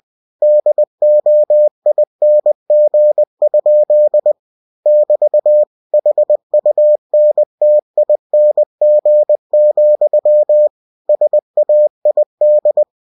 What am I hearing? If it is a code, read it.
Morse code, heard as a keyed tone, about 20 wpm, 600 Hz. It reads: DOING? <BT> HUNTING, SAID
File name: cw-books / winnie-the-pooh / winnie-20wpm-page-0021